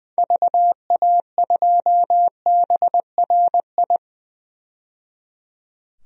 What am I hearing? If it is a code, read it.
Morse code, heard as a keyed tone, about 21 wpm, 700 Hz. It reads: VA2BRI
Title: DXCC-100 Call Signs - 6 Characters - 20wpm